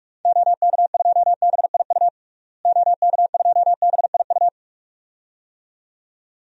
Morse code, heard as a keyed tone, about 46 wpm, 700 Hz. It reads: OK2BIU OK2BIU
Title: DXCC-100 Call Signs - 6 Characters - Repeated 2x - 45wpm